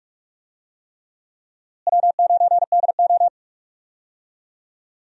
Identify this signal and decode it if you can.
Morse code, heard as a keyed tone, about 46 wpm, 700 Hz. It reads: W9DO
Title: USA Call Signs - 4 Characters - Repeated 3x - 45wpm